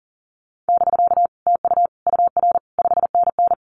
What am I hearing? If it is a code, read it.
Morse code, heard as a keyed tone, about 40 wpm, 700 Hz. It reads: <BK> TU UR 5NN
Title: Common QSO Phrases Encore 40wpm